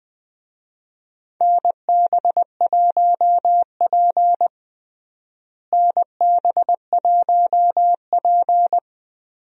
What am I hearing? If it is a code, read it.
Morse code, heard as a keyed tone, about 21 wpm, 700 Hz. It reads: NB1P NB1P
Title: USA Call Signs - 4 Characters - Repeated 3x - 20wpm